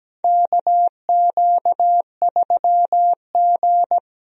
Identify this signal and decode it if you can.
Morse code, heard as a keyed tone, about 17 wpm, 700 Hz. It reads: KQ3G
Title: USA Call Signs - 4 Characters - 17wpm